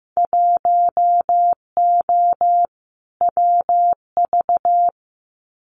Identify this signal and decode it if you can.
Morse code, heard as a keyed tone, about 15 wpm, 700 Hz. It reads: 1O WV
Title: Winter Field Day Exchanges 15wpm